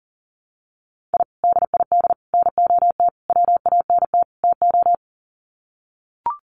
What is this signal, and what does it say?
Morse code, heard as a keyed tone, about 38 wpm, 700 Hz. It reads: I DID NOT WANT TO
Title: Sets of 5 Words 40wpm